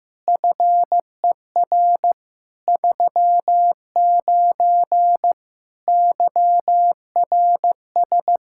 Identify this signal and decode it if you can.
Morse code, heard as a keyed tone, about 15 wpm, 700 Hz. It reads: FER 39 YRS
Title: Practice QSO - Exchange 3 - More Info - 15wpm